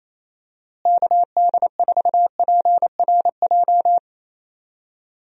Morse code, heard as a keyed tone, about 28 wpm, 700 Hz. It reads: KD4PRJ
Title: USA Call Signs - 6 Characters - Repeated 2x - 28wpm